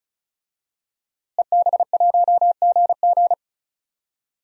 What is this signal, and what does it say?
Morse code, heard as a keyed tone, about 35 wpm, 700 Hz. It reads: EB1GG
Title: DXCC-100 Call Signs - 5 Characters - Repeated 2x - 35wpm